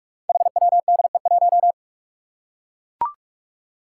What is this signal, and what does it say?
Morse code, heard as a keyed tone, about 44 wpm, 700 Hz. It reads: HWDE1
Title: Five Challenging Letter-Number Combinations 45wpm